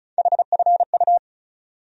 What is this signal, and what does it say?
Morse code, heard as a keyed tone, about 36 wpm, 700 Hz. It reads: HFU
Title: USA Call Sign Postfixes - 3 Letters 35wpm